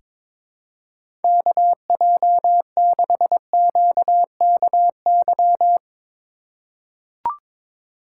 Morse code, heard as a keyed tone, about 22 wpm, 700 Hz. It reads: KJ6QKY
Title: USA Call Signs - 6 Characters - 3x Character Spacing - 22wpm